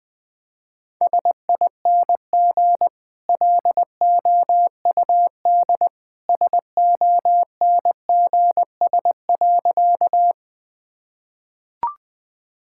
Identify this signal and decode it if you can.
Morse code, heard as a keyed tone, about 20 wpm, 700 Hz. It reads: SING LOUD SONGS.
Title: Easy Sentences with Hints - 20wpm